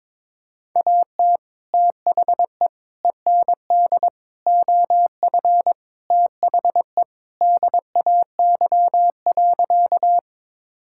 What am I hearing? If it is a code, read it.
Morse code, heard as a keyed tone, about 22 wpm, 700 Hz. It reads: AT THE END OF THE DAY.
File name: practice / 22wpm / Sentences with 2-Word Phrases - 22wpm